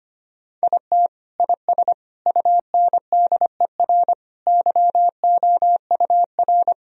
Morse code, heard as a keyed tone, about 25 wpm, 700 Hz. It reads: IT IS UNDER YOUR
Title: Sentences from Top 1000 Words - Review Part 1 - 25wpm